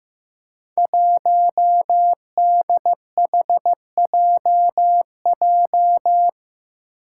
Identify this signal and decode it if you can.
Morse code, heard as a keyed tone, about 15 wpm, 700 Hz. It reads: 1DHJJ
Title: Five Challenging Letter-Number Combinations 15wpm